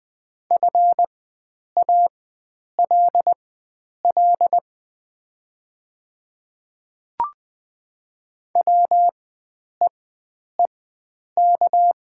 Morse code, heard as a keyed tone, about 20 wpm, spaced as for 9 wpm, 700 Hz. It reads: FALL WEEK
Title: 4x Character Spacing - Top 500 Words - 20wpm